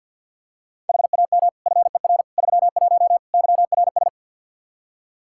Morse code, heard as a keyed tone, about 46 wpm, 700 Hz. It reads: HAM FER 31 YRS
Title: Practice QSO - Exchange 3 - More Info - 50wpm